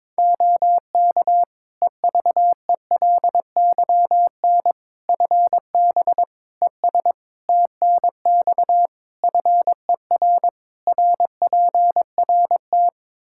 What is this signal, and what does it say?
Morse code, heard as a keyed tone, about 22 wpm, 700 Hz. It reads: OK EVELYN FB ES TNX FER RPRT
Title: Practice QSO - Exchange 2 - Info - 22wpm